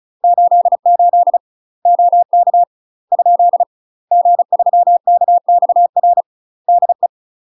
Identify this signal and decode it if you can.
Morse code, heard as a keyed tone, about 35 wpm, 700 Hz. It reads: 88 OK ? G3KXR DE
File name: UK MASTER QSOs TRAINING / 35WPM / UK MASTER QSOs 0023